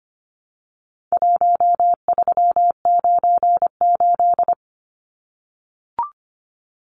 Morse code, heard as a keyed tone, about 25 wpm, 700 Hz. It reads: 1398
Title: Numbers - Contest Style - 4-Digit 25wpm